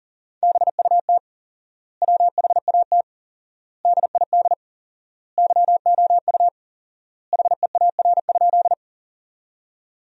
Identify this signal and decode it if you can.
Morse code, heard as a keyed tone, about 40 wpm, 700 Hz. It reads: BUT WHAT DID YOU HEAR?
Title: 4x Word-Spacing - Sentences from Top 300 Words - 40wpm